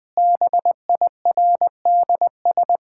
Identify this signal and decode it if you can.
Morse code, heard as a keyed tone, about 21 wpm, 700 Hz. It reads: BIRDS
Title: Sentences with 3-Word Phrases - 20wpm